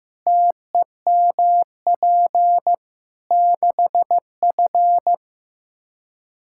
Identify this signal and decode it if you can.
Morse code, heard as a keyed tone, about 16 wpm, 700 Hz. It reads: TEMP 6F
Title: Practice QSO - Exchange 2 - Info - 15wpm